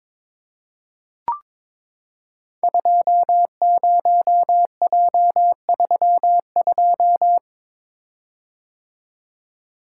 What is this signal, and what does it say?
Morse code, heard as a keyed tone, about 23 wpm, 700 Hz. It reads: 20J32
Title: Code Groups - 5 Letter:Numbers - 22wpm